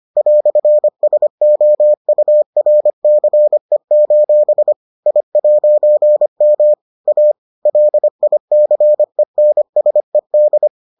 Morse code, heard as a keyed tone, about 25 wpm, 600 Hz. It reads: "SOURCE: I'M A LICENSED